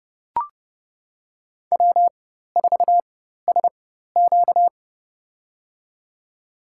Morse code, heard as a keyed tone, about 30 wpm, spaced as for 14 wpm, 700 Hz. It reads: W4SQ
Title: USA Call Signs - 4x Character Spacing - 30wpm